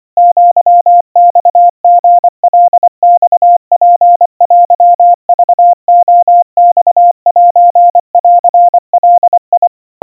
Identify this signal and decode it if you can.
Morse code (Japanese Wabun code), heard as a keyed tone, about 25 wpm, 700 Hz. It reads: アマリカマツテクレマセンガ
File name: 90 cpm 700 Hz Crispy